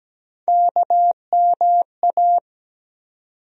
Morse code, heard as a keyed tone, about 17 wpm, 700 Hz. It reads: KMA
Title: USA Call Sign Postfixes - 3 Letters 17wpm